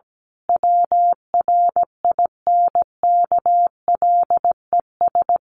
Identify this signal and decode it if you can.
Morse code, heard as a keyed tone, about 17 wpm, 700 Hz. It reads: WRINKLES
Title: A Sound Of Thunder 17wpm